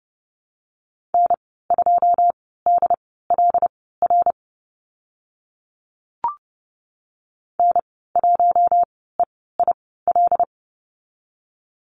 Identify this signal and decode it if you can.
Morse code, heard as a keyed tone, about 30 wpm, spaced as for 17 wpm, 700 Hz. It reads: N2DLR N1EIL
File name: Lesson 14g - Number 2 - Call Signs 30wpm